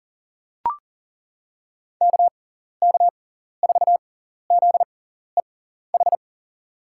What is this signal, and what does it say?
Morse code, heard as a keyed tone, about 40 wpm, spaced as for 14 wpm, 700 Hz. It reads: KK4ZEH
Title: USA Call Signs - 6 Characters - 6x Character Spacing - 40wpm